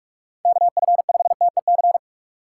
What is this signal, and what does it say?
Morse code, heard as a keyed tone, about 46 wpm, 700 Hz. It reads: KF5TEC
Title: USA Call Signs - 4x Character Spacing - 45wpm